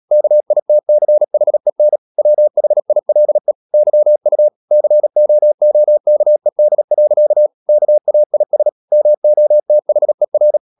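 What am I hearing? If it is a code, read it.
Morse code, heard as a keyed tone, about 37 wpm, 600 Hz. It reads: KITCHEN WHILE YU COOKED. KAIS MOTHER
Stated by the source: Morse code practice at 35–39 wpm sending English text of Ken Follett's novel Never, from CW med Gnister.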